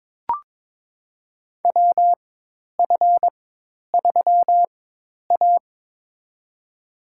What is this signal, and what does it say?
Morse code, heard as a keyed tone, about 22 wpm, spaced as for 10 wpm, 700 Hz. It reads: WF3A
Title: USA Call Signs - 4x Character Spacing - 22wpm